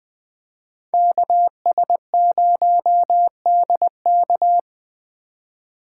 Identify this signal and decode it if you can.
Morse code, heard as a keyed tone, about 22 wpm, 700 Hz. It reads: KS0DK
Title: Speed Racing - USA 2x2 Call Signs 20wpm